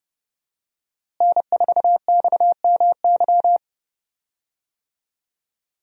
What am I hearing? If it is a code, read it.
Morse code, heard as a keyed tone, about 29 wpm, 700 Hz. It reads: N4XMY
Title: USA Call Signs - 5 Characters - Repeated 3x - 30wpm